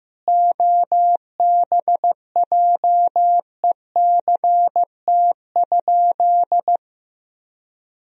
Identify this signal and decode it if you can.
Morse code, heard as a keyed tone, about 15 wpm, 700 Hz. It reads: OBJECT?
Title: Sentences from Top 1000 Words - Review Part 3 - 15wpm